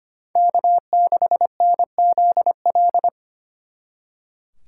Morse code, heard as a keyed tone, about 30 wpm, 700 Hz. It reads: K6NZL
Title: Speed Racing - USA 1x3 Call Signs 25wpm